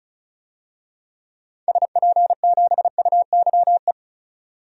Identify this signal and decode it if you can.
Morse code, heard as a keyed tone, about 34 wpm, 700 Hz. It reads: SP7UYE
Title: DXCC-100 Call Signs - 6 Characters - 35wpm